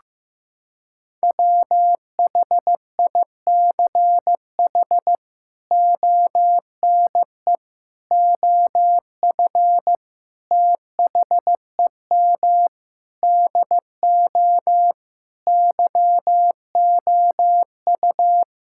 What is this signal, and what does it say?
Morse code, heard as a keyed tone, about 15 wpm, 700 Hz. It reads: WHICH ONE OF THEM DO YOU
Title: Sentences from Top 1000 Words - Review Part 5 - 15wpm